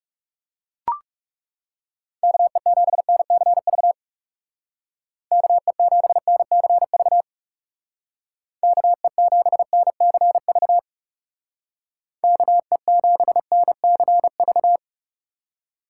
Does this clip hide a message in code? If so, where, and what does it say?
Yes: Morse code from 2.2 to 14.8 s, heard as a keyed tone, about 40 wpm, 700 Hz. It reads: KE7NCV KE7NCV KE7NCV KE7NCV